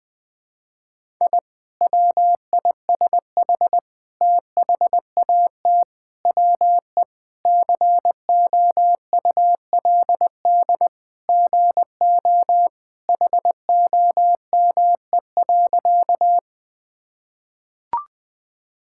Morse code, heard as a keyed tone, about 21 wpm, 700 Hz. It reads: I WISH THAT WE COULD GO HOME.
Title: Sentences with 2-Word Phrases - 20wpm